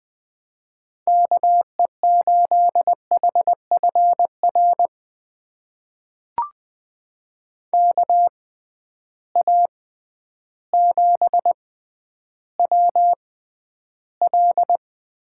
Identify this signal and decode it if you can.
Morse code, heard as a keyed tone, about 20 wpm, spaced as for 11 wpm, 700 Hz. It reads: KE8HFR KA7WL
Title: USA Call Signs - 6 Characters - 6x Character Spacing - 20wpm